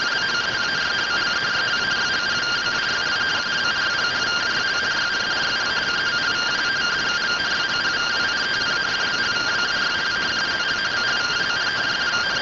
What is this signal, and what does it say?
Two-tone teleprinter signal, about 170 Hz shift.